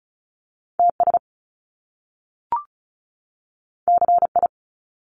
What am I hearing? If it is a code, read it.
Morse code, heard as a keyed tone, about 36 wpm, 700 Hz. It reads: TS CI
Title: Top Two Letter Combinations in Words 35wpm